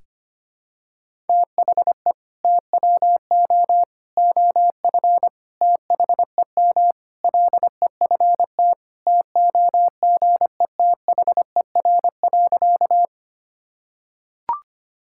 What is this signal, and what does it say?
Morse code, heard as a keyed tone, about 25 wpm, 700 Hz. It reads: THE TWO OF THEM LEFT TOGETHER.
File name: practice / 25wpm / Sentences with 4-Word Phrases - 25wpm